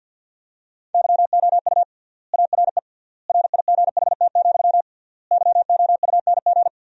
Morse code, heard as a keyed tone, about 50 wpm, 700 Hz. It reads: YOU ARE RIGHT, YOUNG
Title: 3x Word-Spacing - Sentences from Top 400 Words - 50wpm